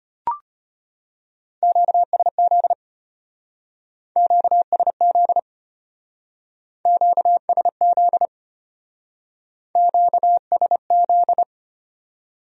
Morse code, heard as a keyed tone, about 32 wpm, 700 Hz. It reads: QSZ QSZ QSZ QSZ